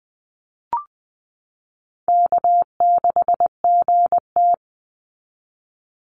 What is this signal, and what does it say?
Morse code, heard as a keyed tone, about 20 wpm, 700 Hz. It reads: K6GT
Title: Easy Call Signs Repeated 3x - 20wpm